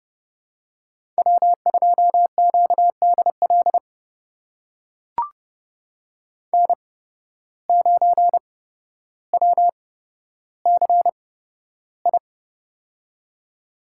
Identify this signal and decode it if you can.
Morse code, heard as a keyed tone, about 30 wpm, spaced as for 13 wpm, 700 Hz. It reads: W2QDL N9WCI
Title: USA Call Signs - 5 Characters - 8x Character Spacing - 30wpm